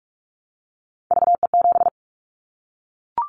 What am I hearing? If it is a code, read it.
Morse code, heard as a keyed tone, about 44 wpm, 700 Hz. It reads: VE7
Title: Callsign Prefixes for Top 100 DXCC - 45wpm